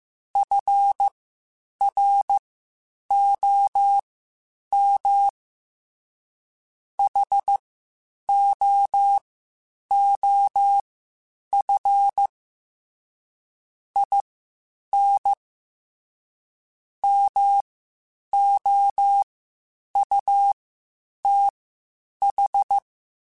Morse code, heard as a keyed tone, about 15 wpm, spaced as for 8 wpm, 800 Hz. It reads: FROM HOOF IN MOUTH
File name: Quotes Spaced / Quotes Spaced-mp3s / HoofInMouth sp @13